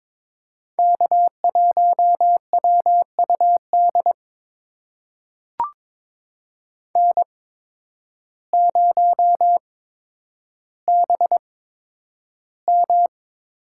Morse code, heard as a keyed tone, about 22 wpm, spaced as for 10 wpm, 700 Hz. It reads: K1WUD N0BM
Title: USA Call Signs - 8x Character Spacing - 22wpm